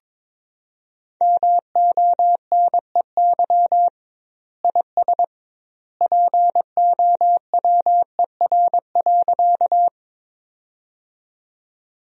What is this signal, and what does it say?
Morse code, heard as a keyed tone, about 22 wpm, 700 Hz. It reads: MONEY IS POWER.
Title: 2x Word-Spacing - Sentences from Top 500 Words - 22wpm